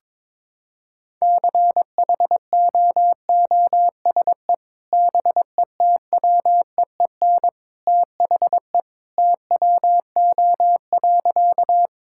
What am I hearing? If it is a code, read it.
Morse code, heard as a keyed tone, about 22 wpm, 700 Hz. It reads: CHOOSE BETWEEN THE TWO.
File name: practice / 22wpm / Sentences with 2-Word Phrases - 22wpm